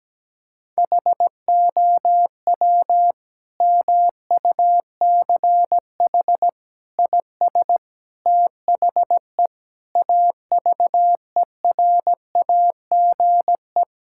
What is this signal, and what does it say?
Morse code, heard as a keyed tone, about 17 wpm, 700 Hz. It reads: HOW MUCH IS THE AVERAGE
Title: Sentences from Top 1000 Words - Review Part 6 - 17wpm